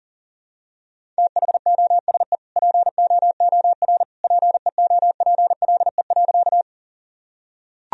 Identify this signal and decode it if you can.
Morse code, heard as a keyed tone, about 37 wpm, 700 Hz. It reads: THOSE POOR PEOPLE.
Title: Sentences from Top 1000 Words - Review Part 1 - 40wpm